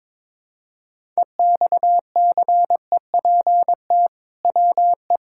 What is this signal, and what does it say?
Morse code, heard as a keyed tone, about 22 wpm, 700 Hz. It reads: EXCEPT WE
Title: Sentences from Top 1600 Words 22wpm